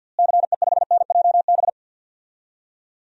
Morse code, heard as a keyed tone, about 52 wpm, 700 Hz. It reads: CE5NJB
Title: DXCC-100 Call Signs - 6 Characters - Repeated 3x - 50wpm